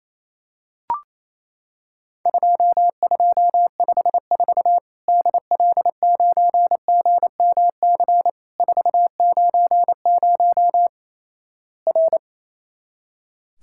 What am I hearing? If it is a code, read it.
Morse code, heard as a keyed tone, about 28 wpm, 700 Hz. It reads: 2254 DL9GMC 490 R
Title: Worked All Europe DX Contest - QTC Exchange - 28wpm